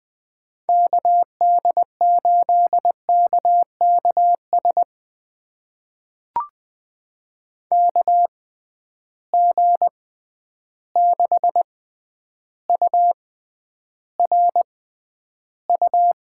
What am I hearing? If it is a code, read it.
Morse code, heard as a keyed tone, about 20 wpm, spaced as for 10 wpm, 700 Hz. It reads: KD8KKS KG6URU
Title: USA Call Signs - 6 Characters - 6x Character Spacing - 20wpm